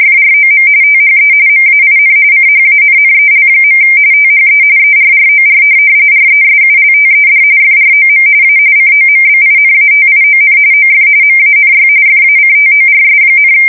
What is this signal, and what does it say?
RTTY, about 45.45 baud. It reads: FROM THE MAIN CHANNEL. THE ERROR RESULTED FROM A COMBINATION OF MANY FACTORS,
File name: fsk170 / USS-Missouri